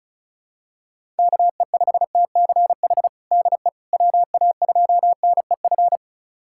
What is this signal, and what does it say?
Morse code, heard as a keyed tone, about 34 wpm, 700 Hz. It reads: KE5TCH DE WA2NEF
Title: USA Callsign DE Callsign 35wpm